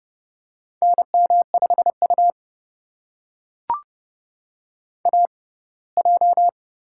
Morse code, heard as a keyed tone, about 30 wpm, spaced as for 20 wpm, 700 Hz. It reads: NM5U AJ
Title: USA Call Signs - 4 Characters - 6x Character Spacing - 30wpm